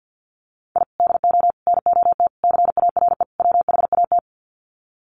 Morse code, heard as a keyed tone, about 50 wpm, 700 Hz. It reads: I DO NOT CARE WHAT
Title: Sets of 5 Words 50wpm